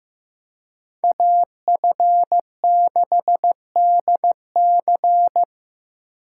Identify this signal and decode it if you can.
Morse code, heard as a keyed tone, about 18 wpm, 700 Hz. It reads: AF6DC
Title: Speed Racing - USA 2x2 Call Signs 15wpm